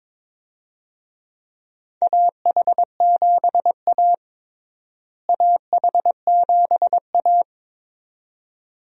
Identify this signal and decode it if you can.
Morse code, heard as a keyed tone, about 22 wpm, 700 Hz. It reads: AH7A AH7A
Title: USA Call Signs - 4 Characters - Repeated 2x - 22wpm